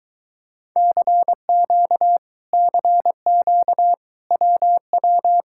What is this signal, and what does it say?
Morse code, heard as a keyed tone, about 23 wpm, 700 Hz. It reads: CQ CQ WW